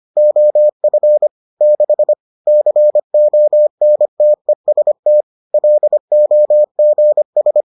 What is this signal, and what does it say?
Morse code, heard as a keyed tone, about 25 wpm, 600 Hz. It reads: OF 6 CONTEST LOGS